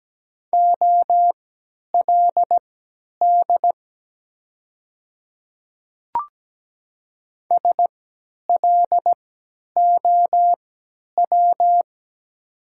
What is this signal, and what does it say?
Morse code, heard as a keyed tone, about 17 wpm, spaced as for 10 wpm, 700 Hz. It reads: OLD SLOW
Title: 3x Character Spacing - Top 500 Words - 17wpm